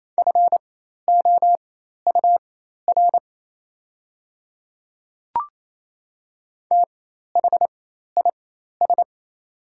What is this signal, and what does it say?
Morse code, heard as a keyed tone, about 28 wpm, spaced as for 13 wpm, 690 Hz. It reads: FOUR THIS
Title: 4x Character Spacing - Top 300 Words - 28wpm